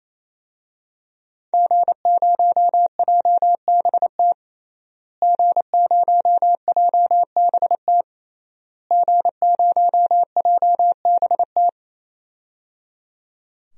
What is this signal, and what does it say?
Morse code, heard as a keyed tone, about 28 wpm, 700 Hz. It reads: G0JBT G0JBT G0JBT
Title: DXCC-100 Call Signs - 5 Characters - Repeated 3x - 28wpm